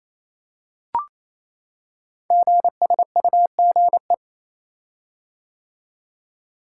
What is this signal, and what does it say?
Morse code, heard as a keyed tone, about 27 wpm, 700 Hz. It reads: GSUGE
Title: Code Groups - 5 Letters - 28wpm